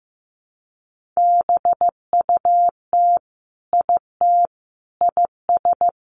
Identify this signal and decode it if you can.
Morse code, heard as a keyed tone, about 15 wpm, 700 Hz. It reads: BUT IT IS
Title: Sentences from Top 1100 Words 15wpm